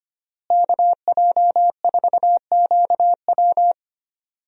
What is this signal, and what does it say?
Morse code, heard as a keyed tone, about 25 wpm, 700 Hz. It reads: KJ4QW
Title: USA Call Signs - 5 Characters - Repeated 3x - 25wpm